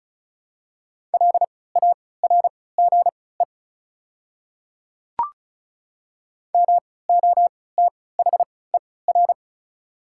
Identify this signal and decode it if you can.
Morse code, heard as a keyed tone, about 36 wpm, spaced as for 20 wpm, 700 Hz. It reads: LARGE MOTHER
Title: 3x Character Spacing - Top 200 Words - 35wpm